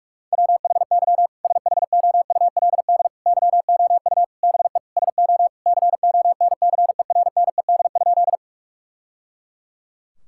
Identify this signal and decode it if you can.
Morse code, heard as a keyed tone, about 41 wpm, 700 Hz. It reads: WHY SHOULD YOU BE SO CONCERNED?
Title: Sentences from Top 1000 Words - Review Part 6 - 45wpm